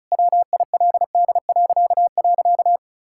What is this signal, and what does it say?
Morse code, heard as a keyed tone, about 36 wpm, 700 Hz. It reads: WILD..
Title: Pangram Challenge v2 - 35wpm